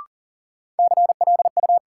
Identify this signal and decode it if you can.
Morse code, heard as a keyed tone, about 40 wpm, 700 Hz. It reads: CLU
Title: Sentences from Top 1200 Words - New Word Drill Alt - 40wpm